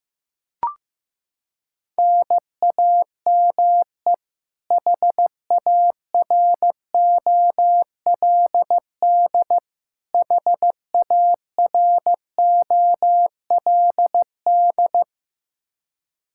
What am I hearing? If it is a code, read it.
Morse code, heard as a keyed tone, about 15 wpm, 700 Hz. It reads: NAME HAROLD HAROLD
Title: Practice QSO - Exchange 1 - Beginning - 15wpm